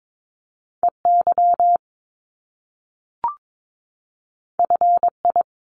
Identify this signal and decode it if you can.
Morse code, heard as a keyed tone, about 22 wpm, 700 Hz. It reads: EY FI
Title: Top Two Letter Combinations in Words 22wpm